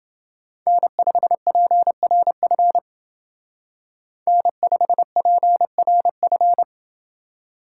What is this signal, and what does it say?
Morse code, heard as a keyed tone, about 28 wpm, 700 Hz. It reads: N5PRF N5PRF